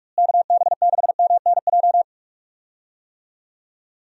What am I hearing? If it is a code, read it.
Morse code, heard as a keyed tone, about 45 wpm, 700 Hz. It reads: KB6MNJ
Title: USA Call Signs - 6 Characters - Repeated 2x - 45wpm